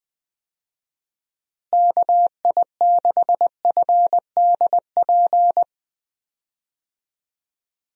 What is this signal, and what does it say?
Morse code, heard as a keyed tone, about 21 wpm, 700 Hz. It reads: KI6FDP
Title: USA Call Signs - 6 Characters - Repeated 3x - 20wpm